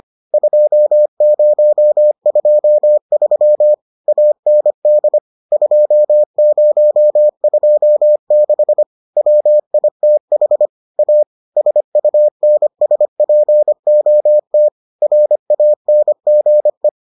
Morse code, heard as a keyed tone, about 25 wpm, 600 Hz. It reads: 2023 AND 2026 WITH A SUNSPOT RANGE